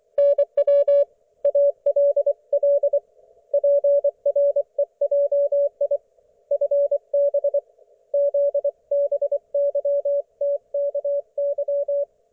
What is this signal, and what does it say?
Morse code, heard as a keyed tone, about 24 wpm, 570 Hz. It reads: NW ALL PREJI FB ZBYTKY
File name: Archiv / CUC224 / ok1fhi cul224